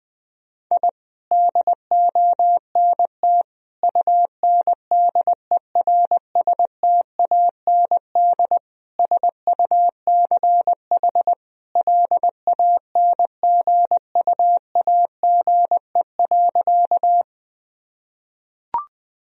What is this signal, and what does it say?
Morse code, heard as a keyed tone, about 20 wpm, 700 Hz. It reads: I DONT UNDERSTAND SUCH LANGUAGE.
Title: Sentences from Top 1000 Words - Review Part 3 - 20wpm